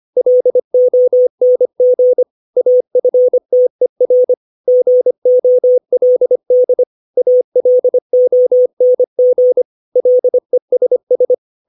Morse code, heard as a keyed tone, about 25 wpm, 500 Hz. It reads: LONG AFTER GOLD ALONG LESS
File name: 632 words / 632 Words 25WPM